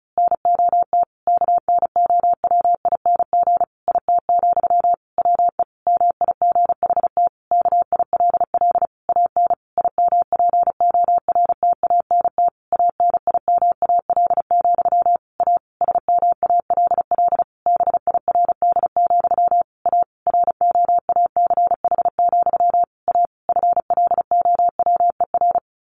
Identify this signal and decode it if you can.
Morse code, heard as a keyed tone, about 34 wpm, 700 Hz. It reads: NOT KNOWING IT, WE MIGHT KILL AN IMPORTANT ANIMAL, A SMALL BIRD, A ROACH, A FLOWER
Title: A Sound Of Thunder 35wpm